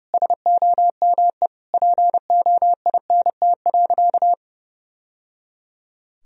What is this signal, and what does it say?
Morse code, heard as a keyed tone, about 30 wpm, 700 Hz. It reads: SOME POINT.
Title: Sentences from Top 1000 Words - Review Part 1 - 30wpm